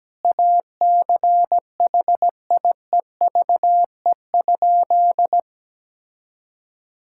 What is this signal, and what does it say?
Morse code, heard as a keyed tone, about 17 wpm, 700 Hz. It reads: ACHIEVE?
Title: Sentences from Top 1000 Words - Review Part 6 - 17wpm